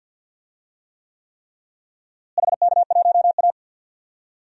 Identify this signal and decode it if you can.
Morse code, heard as a keyed tone, about 51 wpm, 700 Hz. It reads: HK1A
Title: DXCC-100 Call Signs - 4 Characters - Repeated 3x - 50wpm